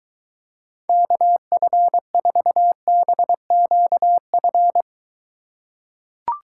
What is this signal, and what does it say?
Morse code, heard as a keyed tone, about 23 wpm, 700 Hz. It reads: KF4BQF